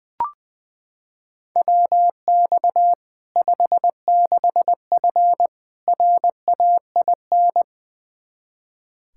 Morse code, heard as a keyed tone, about 20 wpm, 700 Hz. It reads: WX 56F RAIN
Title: Practice QSO - Exchange 2 - Info - 20wpm